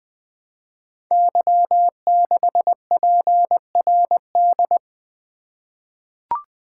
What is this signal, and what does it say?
Morse code, heard as a keyed tone, about 20 wpm, 700 Hz. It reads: Y6PRD
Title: Code Groups - 5 Letter:Numbers - 20wpm